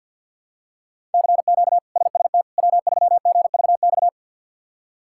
Morse code, heard as a keyed tone, about 46 wpm, 700 Hz. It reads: CQ SST W3GVX
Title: K1USN SST QSO Practice - 50wpm